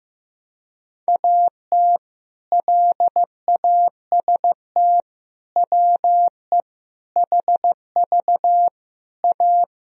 Morse code, heard as a keyed tone, about 15 wpm, 700 Hz. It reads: AT LAST WE HV A
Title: Sentences from Top 300 Words and Abbreviations 15wpm